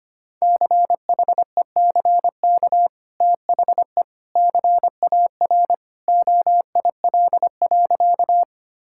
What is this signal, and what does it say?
Morse code, heard as a keyed tone, about 25 wpm, 700 Hz. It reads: CHECK THE CAR OIL.
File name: Easy Sentences with Hints - 25wpm